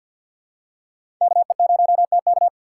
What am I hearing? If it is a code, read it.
Morse code, heard as a keyed tone, about 51 wpm, 700 Hz. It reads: KE0TK
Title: USA Call Signs - 4x Character Spacing - 50wpm